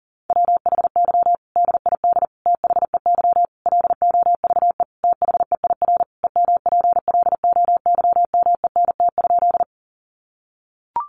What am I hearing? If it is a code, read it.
Morse code, heard as a keyed tone, about 40 wpm, 700 Hz. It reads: WHY DID THEY LOVE THEIR EMPLOYMENT?
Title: Sentences from Top 1500 Words 40wpm